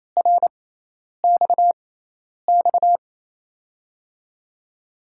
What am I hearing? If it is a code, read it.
Morse code, heard as a keyed tone, about 28 wpm, spaced as for 10 wpm, 700 Hz. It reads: RXX
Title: USA Call Signs - 5 Characters - 6x Character Spacing - 28wpm